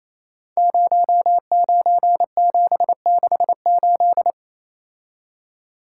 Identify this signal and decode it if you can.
Morse code, heard as a keyed tone, about 29 wpm, 700 Hz. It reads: 09768
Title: Code Groups - 5 Letter:Numbers - 28wpm